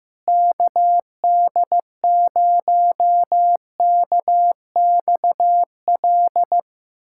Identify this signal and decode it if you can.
Morse code, heard as a keyed tone, about 15 wpm, 700 Hz. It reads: KD0KXL
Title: USA Callsign DE Callsign 15wpm